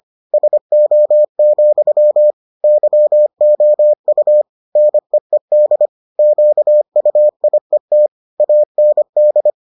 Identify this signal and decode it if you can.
Morse code, heard as a keyed tone, about 25 wpm, 600 Hz. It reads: SO, YOU NEED QUIET AND